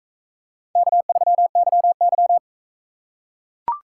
Morse code, heard as a keyed tone, about 42 wpm, 700 Hz. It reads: K3YY